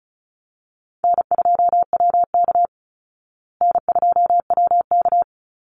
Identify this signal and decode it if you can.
Morse code, heard as a keyed tone, about 34 wpm, 700 Hz. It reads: N2WK N2WK
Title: Call Signs Repeated 3x - 35wpm